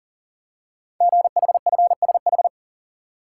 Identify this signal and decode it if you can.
Morse code, heard as a keyed tone, about 39 wpm, 690 Hz. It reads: GHFSH
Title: Code Groups - 5 Letters - 40wpm